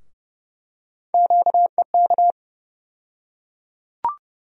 Morse code, heard as a keyed tone, about 30 wpm, 700 Hz. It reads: QEK
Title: USA Call Sign Postfixes - 3 Letters 30wpm